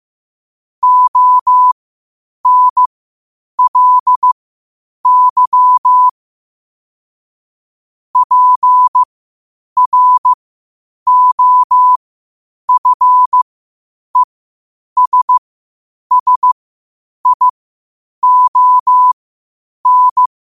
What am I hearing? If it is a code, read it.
Morse code, heard as a keyed tone, about 15 wpm, spaced as for 8 wpm, 1000 Hz. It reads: ONLY PROFESSION
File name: qotdfiles / 8wpm